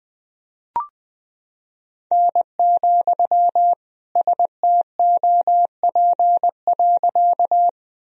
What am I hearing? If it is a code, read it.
Morse code, heard as a keyed tone, about 20 wpm, 700 Hz. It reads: N, STOP.
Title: Sentences from Top 300 Words and Abbreviations 20wpm